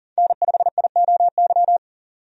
Morse code, heard as a keyed tone, about 40 wpm, 700 Hz. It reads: N5IOY
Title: USA Call Signs - 5 Characters - 40wpm